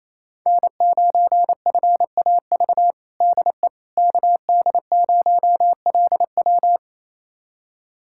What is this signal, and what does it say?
Morse code, heard as a keyed tone, about 28 wpm, 700 Hz. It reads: N9FAV DE KD0LW
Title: USA Callsign DE Callsign - 5 Characters - 28wpm